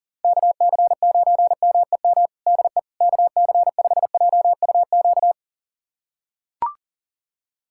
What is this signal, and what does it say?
Morse code, heard as a keyed tone, about 40 wpm, 700 Hz. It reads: KC9MEM DE KC5JUQ
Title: USA Callsign DE Callsign - 6 Characters - 40wpm